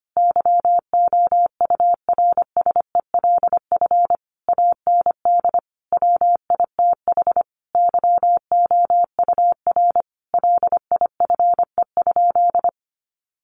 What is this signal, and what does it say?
Morse code, heard as a keyed tone, about 25 wpm, 700 Hz. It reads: YOURSELF AND WITH YOUR LIFE?
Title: Sentences from Top 900 Words 25wpm